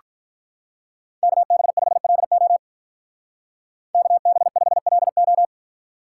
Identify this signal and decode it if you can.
Morse code, heard as a keyed tone, about 43 wpm, 700 Hz. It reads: KB5LO KB5LO
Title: Speed Racing - USA Call Signs 35wpm